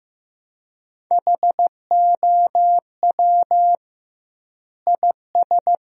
Morse code, heard as a keyed tone, about 15 wpm, 700 Hz. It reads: HOW IS
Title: 2x Word-Spacing - Sentences from Top 100 Words - 15wpm